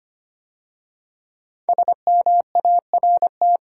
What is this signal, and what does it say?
Morse code, heard as a keyed tone, about 25 wpm, 700 Hz. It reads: SMART
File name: 5 Letter Words - First 3 Letters Given - 25wpm